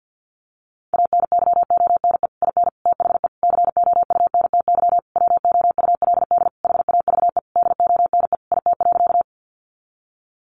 Morse code, heard as a keyed tone, about 50 wpm, 700 Hz. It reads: ANYONE IN THE COUNTY WOULD HAVE DONE IT.
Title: Sentences from Top 1200 Words 50wpm